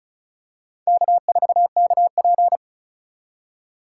Morse code, heard as a keyed tone, about 34 wpm, 700 Hz. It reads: K4KP
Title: USA Call Signs - 2x Character Spacing - 35wpm